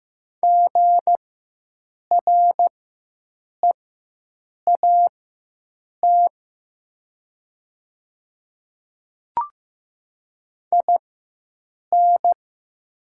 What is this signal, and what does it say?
Morse code, heard as a keyed tone, about 15 wpm, spaced as for 7 wpm, 700 Hz. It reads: GREAT IN
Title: 4x Character Spacing - Top 200 Words - 15wpm